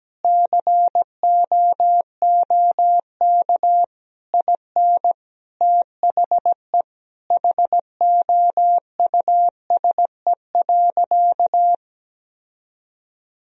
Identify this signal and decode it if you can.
Morse code, heard as a keyed tone, about 17 wpm, 700 Hz. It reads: COOK IN THE HOUSE.
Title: Sentences from Top 1000 Words - Review Part 4 - 17wpm